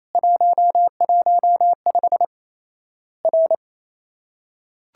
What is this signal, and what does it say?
Morse code, heard as a keyed tone, about 28 wpm, 700 Hz. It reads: 115 R
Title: Worked All Europe DX Contest - QTC Exchange - 28wpm